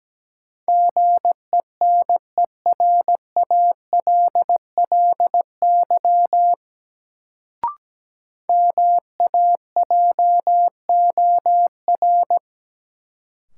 Morse code, heard as a keyed tone, about 17 wpm, 700 Hz. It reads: GENERALLY MAJOR
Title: Sentences from Top 600 Words - New Word Drill - 17wpm